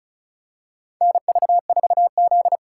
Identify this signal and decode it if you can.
Morse code, heard as a keyed tone, about 34 wpm, 690 Hz. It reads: NV4Z
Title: USA Call Signs - 3x Character Spacing - 35wpm